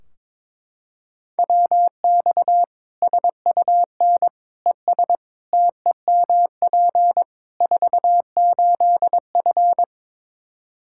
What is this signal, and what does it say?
Morse code, heard as a keyed tone, about 22 wpm, 700 Hz. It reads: WX SUN ES TEMP 48F
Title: K1USN SST QSO Practice 22wpm